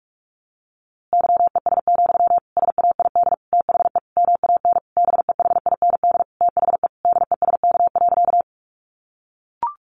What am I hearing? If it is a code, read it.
Morse code, heard as a keyed tone, about 44 wpm, 700 Hz. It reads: YES, SAID THE MAN BEHIND THE DESK.
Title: A Sound Of Thunder 45wpm